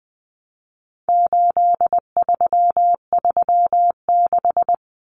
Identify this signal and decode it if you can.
Morse code, heard as a keyed tone, about 20 wpm, 700 Hz. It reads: 8336
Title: Common Cut Numbers - 4-Digit 20wpm